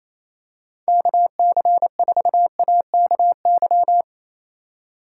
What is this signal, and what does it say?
Morse code, heard as a keyed tone, about 28 wpm, 700 Hz. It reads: KC4AKY
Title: Speed Racing - USA 2x3 Call Signs 28wpm